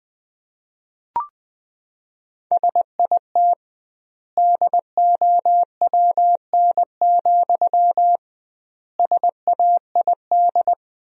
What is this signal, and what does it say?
Morse code, heard as a keyed tone, about 20 wpm, 700 Hz. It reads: SIT DOWN, SAID
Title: 2x Word-Spacing - Sentences from Top 500 Words - 20wpm